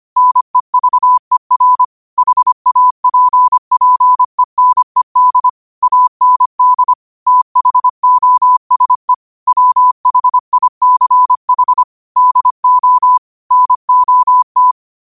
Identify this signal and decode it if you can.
Morse code, heard as a keyed tone, about 25 wpm, 1000 Hz. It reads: NEVER HAPPENED AND THOSE WHICH DO NOT
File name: qotdfiles / 25wpm